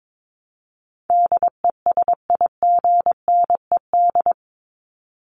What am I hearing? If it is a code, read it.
Morse code, heard as a keyed tone, about 22 wpm, 700 Hz. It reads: DESIGNED
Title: ED Suffix - Top 100 Words with Base Words 22wpm